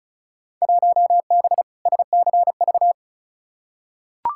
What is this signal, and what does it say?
Morse code, heard as a keyed tone, about 36 wpm, 700 Hz. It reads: 1B SCV
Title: Field Day Exchanges - 35wpm